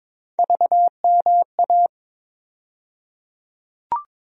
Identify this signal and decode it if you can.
Morse code, heard as a keyed tone, about 22 wpm, 700 Hz. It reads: VMA
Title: USA Call Sign Postfixes - 3 Letters 22wpm